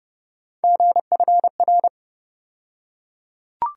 Morse code, heard as a keyed tone, about 30 wpm, 700 Hz. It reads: GFR
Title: USA Call Sign Postfixes - 3 Letters 30wpm